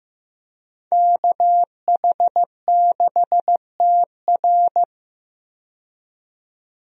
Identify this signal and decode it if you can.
Morse code, heard as a keyed tone, about 15 wpm, 700 Hz. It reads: KH6TR
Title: DXCC-100 Call Signs - 5 Characters - 15wpm